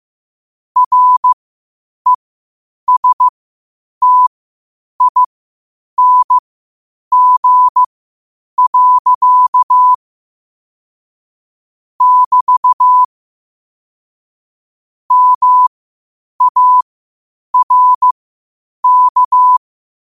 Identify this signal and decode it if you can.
Morse code, heard as a keyed tone, about 15 wpm, spaced as for 8 wpm, 1000 Hz. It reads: RESTING. <BT> MARK
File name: qotdfiles / 8wpm